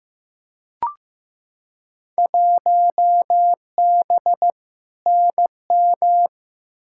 Morse code, heard as a keyed tone, about 15 wpm, 700 Hz. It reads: 1B NM
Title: Field Day Exchanges Encore - 15wpm